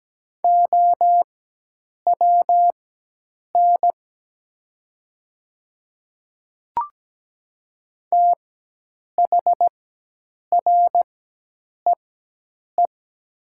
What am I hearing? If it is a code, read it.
Morse code, heard as a keyed tone, about 17 wpm, spaced as for 8 wpm, 700 Hz. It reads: OWN THREE
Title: 4x Character Spacing - Top 500 Words - 17wpm